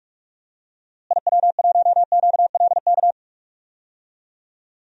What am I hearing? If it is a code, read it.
Morse code, heard as a keyed tone, about 46 wpm, 700 Hz. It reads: IW1QLK
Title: DXCC-100 Call Signs - 6 Characters - Repeated 2x - 45wpm